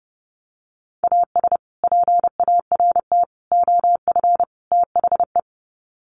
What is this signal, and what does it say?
Morse code, heard as a keyed tone, about 30 wpm, 700 Hz. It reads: AS PART OF THE
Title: Sets of 4 Words 30wpm